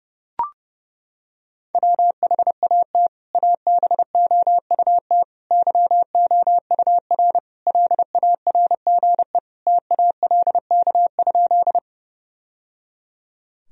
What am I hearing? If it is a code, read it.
Morse code, heard as a keyed tone, about 32 wpm, 700 Hz. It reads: WHAT ABOUT YOUR LARGE TALK?
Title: Sentences from Top 1000 Words - Review Part 1 - 30wpm